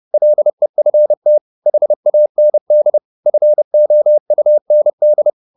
Morse code, heard as a keyed tone, about 30 wpm, 600 Hz. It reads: LEFT HAND FOUND